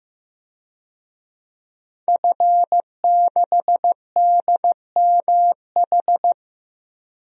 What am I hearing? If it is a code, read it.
Morse code, heard as a keyed tone, about 15 wpm, 700 Hz. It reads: F6DMH
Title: DXCC-100 Call Signs - 5 Characters - Repeated 3x - 15wpm